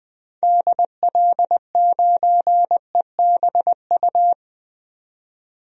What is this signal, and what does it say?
Morse code, heard as a keyed tone, about 20 wpm, 700 Hz. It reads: DL9EBU
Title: DXCC-100 Call Signs - 6 Characters - 20wpm